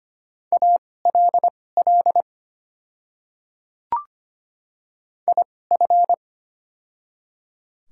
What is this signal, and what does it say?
Morse code, heard as a keyed tone, about 25 wpm, spaced as for 18 wpm, 690 Hz. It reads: ALL IF
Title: 2x Character Spacing - Top 100 Words - 25wpm